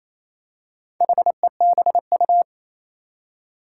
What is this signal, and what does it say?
Morse code, heard as a keyed tone, about 28 wpm, 700 Hz. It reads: HEBU
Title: Four Challenging Letter-Number Combinations 28wpm